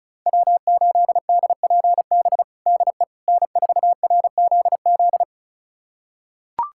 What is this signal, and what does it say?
Morse code, heard as a keyed tone, about 34 wpm, 690 Hz. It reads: W8DPB DE N4RZZ
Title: USA Callsign DE Callsign - 5 Characters - 35wpm